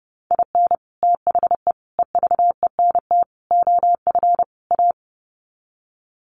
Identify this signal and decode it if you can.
Morse code, heard as a keyed tone, about 30 wpm, 700 Hz. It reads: IN THE EVENT OF A
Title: Sets of 5 Words Encore 30wpm